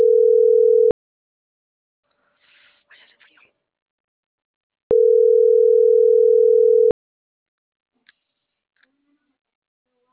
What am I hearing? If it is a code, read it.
Ringback tone (North America), 440 + 480 Hz.